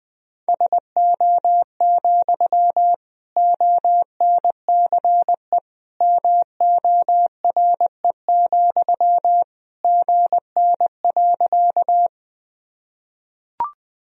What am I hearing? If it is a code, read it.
Morse code, heard as a keyed tone, about 21 wpm, 700 Hz. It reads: SO, ONCE MORE, GN.
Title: Sentences from Top 300 Words and Abbreviations 20wpm